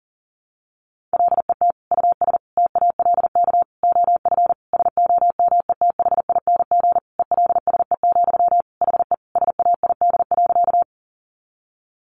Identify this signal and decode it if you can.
Morse code, heard as a keyed tone, about 40 wpm, 700 Hz. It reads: LET US TALK OF SOMETHING ELSE, HE SAID.
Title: Sentences from Top 800 Words 40wpm